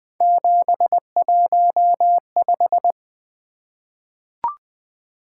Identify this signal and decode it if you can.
Morse code, heard as a keyed tone, about 20 wpm, 700 Hz. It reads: 715
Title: Three Letter-Number Combinations - 20wpm